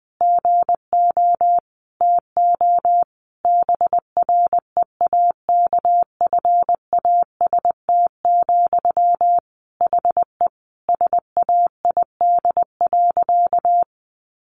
Morse code, heard as a keyed tone, about 20 wpm, 700 Hz. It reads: GO TO BREAKFAST, HE SAID.
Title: Sentences from Top 1500 Words 20wpm